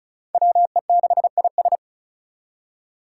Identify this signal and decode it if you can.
Morse code, heard as a keyed tone, about 36 wpm, 700 Hz. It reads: WE6IS
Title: Five Challenging Letter-Number Combinations 35wpm